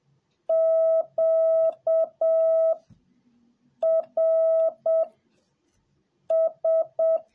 Morse code, heard as a keyed tone, about 7 wpm, 650 Hz. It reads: Q R S